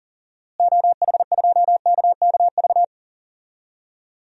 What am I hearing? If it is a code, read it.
Morse code, heard as a keyed tone, about 37 wpm, 700 Hz. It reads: OH2KKV
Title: DXCC-100 Call Signs - 6 Characters - Repeated 3x - 40wpm